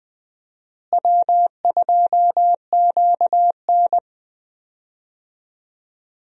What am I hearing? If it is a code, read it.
Morse code, heard as a keyed tone, about 20 wpm, 700 Hz. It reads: W2QN
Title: USA Call Signs - 4 Characters - Repeated 3x - 20wpm